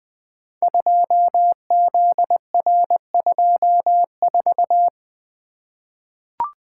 Morse code, heard as a keyed tone, about 20 wpm, 700 Hz. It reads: 2ZR24
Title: Code Groups - 5 Letter:Numbers - 20wpm